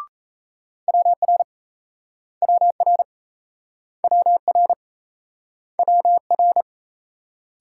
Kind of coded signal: Morse code

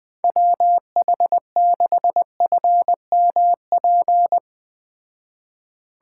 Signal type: Morse code

